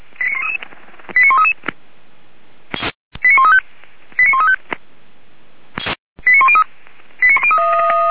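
Five-tone selective call (ZVEI), heard as quick runs of five tones, tones of about 80 ms. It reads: 89133 89135 89135 89103
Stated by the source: ZVEI five-tone call received by SDR, fire brigade Feuerwehr Orlamünde.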